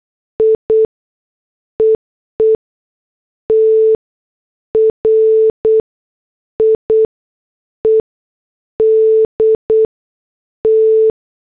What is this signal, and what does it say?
Morse code, heard as a keyed tone, about 8 wpm, spaced as for 6 wpm, 440 Hz. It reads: I EE TRIED T